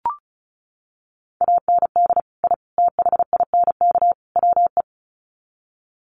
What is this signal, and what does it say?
Morse code, heard as a keyed tone, about 34 wpm, 700 Hz. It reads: AND I THINK WE
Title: Sets of 4 Words 35wpm